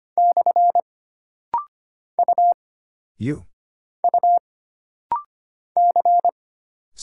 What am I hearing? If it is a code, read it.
Morse code, heard as a keyed tone, about 25 wpm, 700 Hz. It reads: / U U C
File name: Single Character (Letter:Number:Punctuation:Prosign) - Rapid-Fire - 25wpm